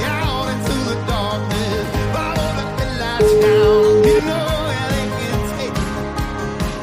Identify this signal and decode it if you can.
Ringback tone (Europe, 425 Hz), with noise.